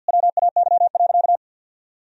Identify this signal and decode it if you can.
Morse code, heard as a keyed tone, about 50 wpm, 700 Hz. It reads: WAY.
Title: Sentences from Top 1000 Words - Review Part 4 - 50wpm